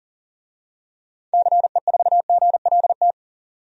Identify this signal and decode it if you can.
Morse code, heard as a keyed tone, about 40 wpm, 700 Hz. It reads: CE4GLT
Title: DXCC-100 Call Signs - 6 Characters - 40wpm